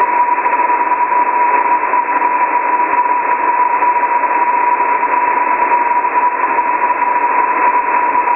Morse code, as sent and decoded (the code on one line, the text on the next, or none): none